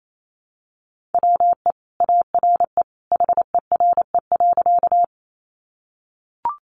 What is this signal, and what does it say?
Morse code, heard as a keyed tone, about 28 wpm, 700 Hz. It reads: WE ARE HERE.
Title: Sentences from Top 200 Words 28wpm